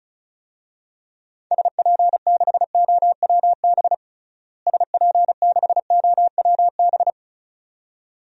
Morse code, heard as a keyed tone, about 34 wpm, 700 Hz. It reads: SP6OWB SP6OWB
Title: DXCC-100 Call Signs - 6 Characters - Repeated 2x - 35wpm